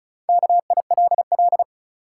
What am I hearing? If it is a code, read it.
Morse code, heard as a keyed tone, about 36 wpm, 700 Hz. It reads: KILL
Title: Sentences from Top 1400 Words - New Word Drill - 35wpm